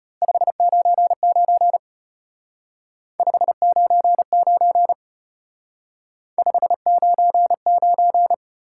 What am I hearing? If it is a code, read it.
Morse code, heard as a keyed tone, about 32 wpm, 700 Hz. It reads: 599 599 599